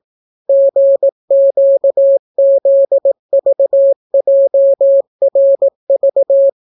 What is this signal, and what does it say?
Morse code, heard as a keyed tone, about 18 wpm, 550 Hz. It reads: GQZVJRV